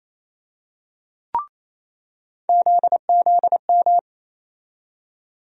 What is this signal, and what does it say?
Morse code, heard as a keyed tone, about 28 wpm, 700 Hz. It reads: ZZM
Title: USA Call Sign Postfixes - 3 Letters 28wpm